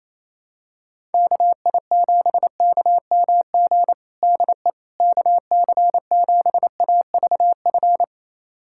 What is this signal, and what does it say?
Morse code, heard as a keyed tone, about 28 wpm, 700 Hz. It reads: KI7KMG DE KC7AVF
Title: USA Callsign DE Callsign - 6 Characters - 28wpm